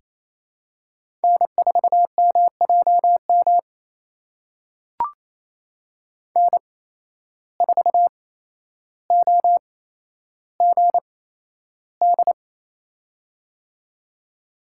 Morse code, heard as a keyed tone, about 28 wpm, spaced as for 12 wpm, 700 Hz. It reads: N4MJM N4OGD
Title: USA Call Signs - 5 Characters - 8x Character Spacing - 28wpm